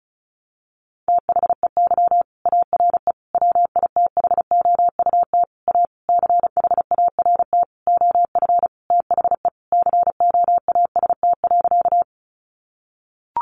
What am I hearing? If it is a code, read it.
Morse code, heard as a keyed tone, about 34 wpm, 700 Hz. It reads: THEY ARE WITHOUT A CHART OF THE COAST.
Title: Sentences from Top 1300 Words 35wpm